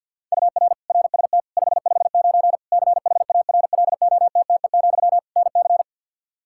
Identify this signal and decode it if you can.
Morse code, heard as a keyed tone, about 52 wpm, 700 Hz. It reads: UR RST 559 CHARLOTTE, NC
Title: Practice QSO - Exchange 1 - Beginning - 50wpm